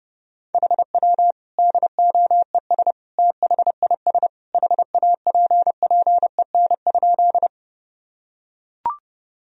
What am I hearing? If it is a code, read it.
Morse code, heard as a keyed tone, about 30 wpm, 700 Hz. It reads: HW DOES THIS HAPPEN?
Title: Sentences from Top 400 Words and Abbreviations 30wpm